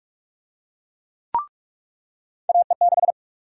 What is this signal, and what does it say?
Morse code, heard as a keyed tone, about 42 wpm, 700 Hz. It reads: AE6
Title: USA Call Sign Prefixes - 45wpm